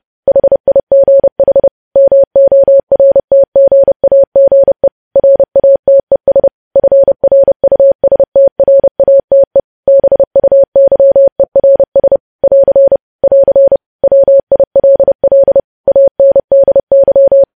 Morse code, heard as a keyed tone, about 30 wpm, 550 Hz. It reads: HIGH MORTGAGE RATES FRUSTRATE BUYERS <AR> <AR> WILL ANDY
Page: news-headlines-in-morse-code-at-30-wpm